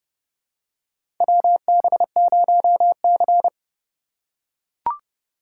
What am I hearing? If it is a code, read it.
Morse code, heard as a keyed tone, about 29 wpm, 700 Hz. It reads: WB0C
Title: USA Call Signs - 4 Characters - 3x Character Spacing - 30wpm